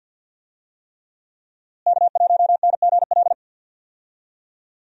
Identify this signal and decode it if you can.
Morse code, heard as a keyed tone, about 51 wpm, 700 Hz. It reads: K1NGL
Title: USA Call Signs - 5 Characters - Repeated 3x - 50wpm